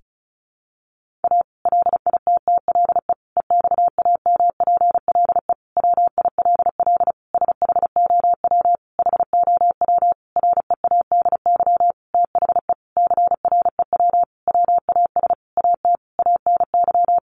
Morse code, heard as a keyed tone, about 34 wpm, 700 Hz. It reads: A LITTLE EXAMPLE WILL SHOW HOW READY THE CREW WAS AT ANY
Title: Sentences from Top 1500 Words 35wpm